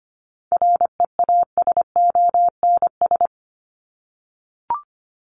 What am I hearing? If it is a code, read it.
Morse code, heard as a keyed tone, about 25 wpm, 700 Hz. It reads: REASONS
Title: ES:S Suffix - Top 100 Words with Base Words 25wpm